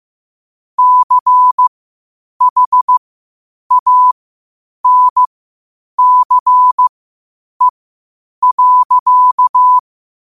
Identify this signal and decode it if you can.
Morse code, heard as a keyed tone, about 15 wpm, spaced as for 8 wpm, 1000 Hz. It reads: CHANCE.
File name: qotdfiles / 8wpm